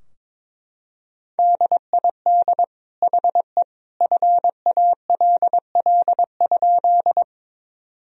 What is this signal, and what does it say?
Morse code, heard as a keyed tone, about 22 wpm, 700 Hz. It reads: DID HE FALL?
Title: Sentences from Top 1000 Words - Review Part 2 - 22wpm